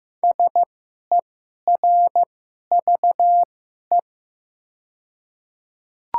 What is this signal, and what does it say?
Morse code, heard as a keyed tone, about 15 wpm, spaced as for 11 wpm, 690 Hz. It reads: SERVE
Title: 2x Character Spacing - Top 500 Words - 15wpm